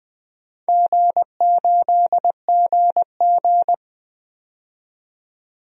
Morse code, heard as a keyed tone, about 20 wpm, 700 Hz. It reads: G8GG
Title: DXCC-100 Call Signs - 4 Characters - 20wpm